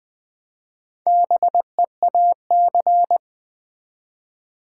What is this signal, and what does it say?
Morse code, heard as a keyed tone, about 20 wpm, 700 Hz. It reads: BEAC
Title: Build A Word - 1 Letter At A Time - 3 to 5-Letter Words - 20wpm